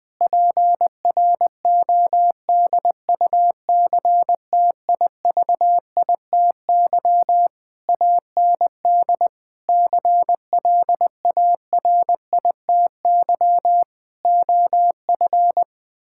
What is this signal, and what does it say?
Morse code, heard as a keyed tone, about 20 wpm, 700 Hz. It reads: PRODUCTIVITY AND CLARITY OF
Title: Difficult Sentences with Hints - 20wpm